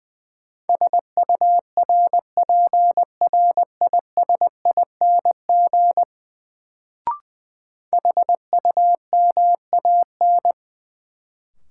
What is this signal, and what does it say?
Morse code, heard as a keyed tone, about 20 wpm, 700 Hz. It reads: SURPRISING HUMAN
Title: Sets of 5 Words - New Word Drill - 20wpm